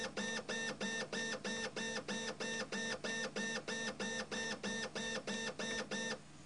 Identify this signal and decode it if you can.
There is no code.